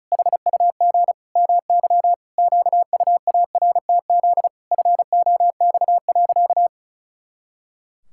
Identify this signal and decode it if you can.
Morse code, heard as a keyed tone, about 34 wpm, 700 Hz. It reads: HUG MY QUARTZ FOX.
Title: Pangram Challenge v2 - 35wpm